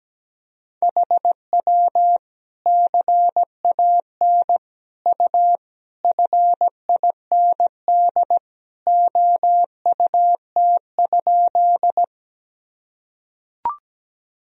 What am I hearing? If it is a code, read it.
Morse code, heard as a keyed tone, about 17 wpm, 700 Hz. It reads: HW CAN U FIND OUT?
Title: Sentences from Top 200 Words and Abbreviations 17wpm